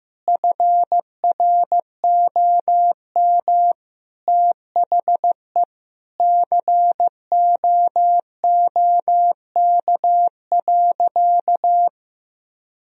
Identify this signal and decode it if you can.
Morse code, heard as a keyed tone, about 15 wpm, 700 Hz. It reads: FROM THE COOK.
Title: Sentences from Top 1000 Words - Review Part 4 - 15wpm